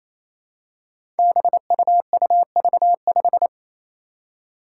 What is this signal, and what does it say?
Morse code, heard as a keyed tone, about 28 wpm, 700 Hz. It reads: BUUV5
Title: Five Challenging Letter-Number Combinations 28wpm